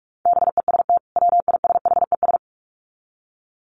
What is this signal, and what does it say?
Morse code, heard as a keyed tone, about 44 wpm, 700 Hz. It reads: BEST WISHES
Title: Common QSO Phrases 45wpm